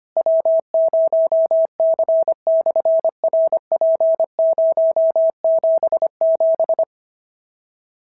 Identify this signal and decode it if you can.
Morse code, heard as a keyed tone, about 25 wpm, 650 Hz. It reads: W0C/RP077
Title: SOTA QSO Practice 25wpm